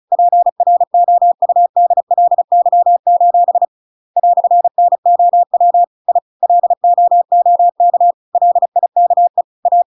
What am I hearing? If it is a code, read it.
Morse code, heard as a keyed tone, about 34 wpm, 700 Hz. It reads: PROUDLY: "NOW I LOOK LIKE A
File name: Pinocchio / Pinocchio ch8 35 35 ik1hge